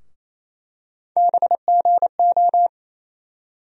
Morse code, heard as a keyed tone, about 27 wpm, 700 Hz. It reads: BGO
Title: USA Call Sign Postfixes - 3 Letters 28wpm